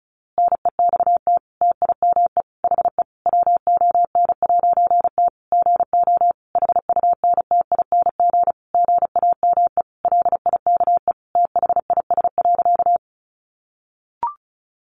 Morse code, heard as a keyed tone, about 34 wpm, 700 Hz. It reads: NEXT TIME HE WON'T GO HUNTING GAME LIKE THIS.
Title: A Sound Of Thunder 35wpm